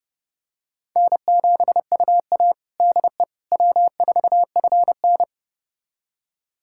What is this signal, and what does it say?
Morse code, heard as a keyed tone, about 32 wpm, 700 Hz. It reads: N7UA DE W4FN
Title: USA Callsign DE Callsign - 4 Characters - 30wpm